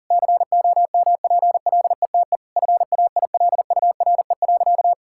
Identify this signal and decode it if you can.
Morse code, heard as a keyed tone, about 40 wpm, 700 Hz. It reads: COMPLETE FAILURE.
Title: Sentences from Top 1000 Words - Review Part 6 - 40wpm